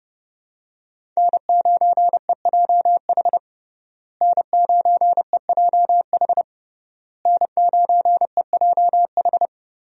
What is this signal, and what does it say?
Morse code, heard as a keyed tone, about 30 wpm, 700 Hz. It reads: N9EJH N9EJH N9EJH
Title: USA Call Signs - 5 Characters - Repeated 3x - 30wpm